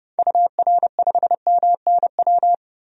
Morse code, heard as a keyed tone, about 32 wpm, 700 Hz. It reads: UR5MNW
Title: DXCC-100 Call Signs - 6 Characters - 30wpm